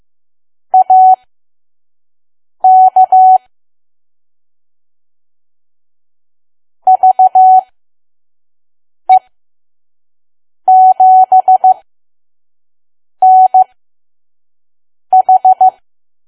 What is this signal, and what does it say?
Morse code, heard as a keyed tone, about 15 wpm, spaced as for 5 wpm, 750 Hz. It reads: AK VE7NH
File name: Archive / 5 WPM files / PM